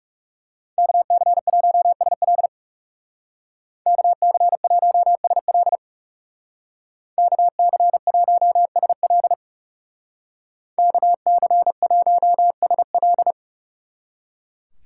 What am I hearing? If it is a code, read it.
Morse code, heard as a keyed tone, about 36 wpm, 700 Hz. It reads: KC1SL KC1SL KC1SL KC1SL
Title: Speed Racing - USA Call Signs 30wpm